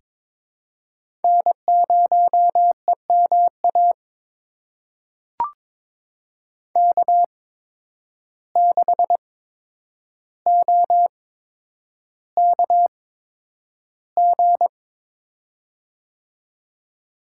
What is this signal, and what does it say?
Morse code, heard as a keyed tone, about 23 wpm, spaced as for 10 wpm, 700 Hz. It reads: N0EMA K6OKG
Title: USA Call Signs - 5 Characters - 8x Character Spacing - 22wpm